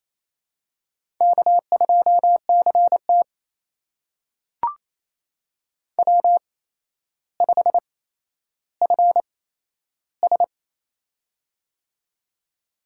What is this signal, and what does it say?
Morse code, heard as a keyed tone, about 28 wpm, spaced as for 12 wpm, 700 Hz. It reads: K2CT W5FS
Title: USA Call Signs - 4 Characters - 8x Character Spacing - 28wpm